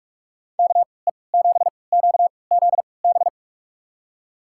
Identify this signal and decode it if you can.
Morse code, heard as a keyed tone, about 46 wpm, spaced as for 26 wpm, 700 Hz. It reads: KE7QZB
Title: USA Call Signs - 3x Character Spacing - 45wpm